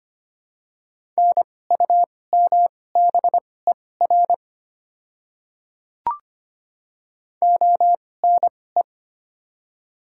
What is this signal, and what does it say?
Morse code, heard as a keyed tone, about 25 wpm, spaced as for 18 wpm, 700 Hz. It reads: NUMBER ONE
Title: 2x Character Spacing - Top 100 Words - 25wpm